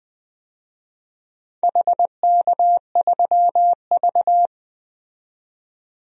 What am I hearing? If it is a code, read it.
Morse code, heard as a keyed tone, about 20 wpm, 700 Hz. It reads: HK3V
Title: DXCC-100 Call Signs - 4 Characters - 20wpm